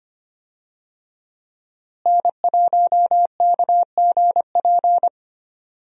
Morse code, heard as a keyed tone, about 25 wpm, 700 Hz. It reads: N1KGP
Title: USA Call Signs - 5 Characters - Repeated 3x - 25wpm